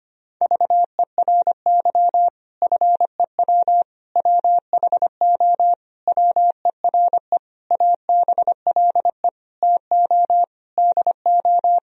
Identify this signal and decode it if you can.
Morse code, heard as a keyed tone, about 25 wpm, 700 Hz. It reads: VERY FEW WHO WERE ABLE TO DO
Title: Sentences from Top 1000 Words - Review Part 2 - 25wpm